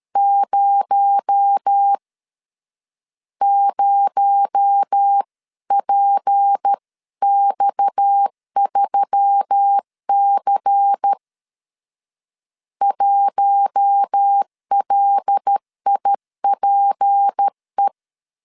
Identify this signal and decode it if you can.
Morse code, heard as a keyed tone, about 13 wpm, 800 Hz. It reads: 0 0PX3C 1LIPE
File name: LEZIONI / 6 PKX0 / 6 PKX0 12wpm 2231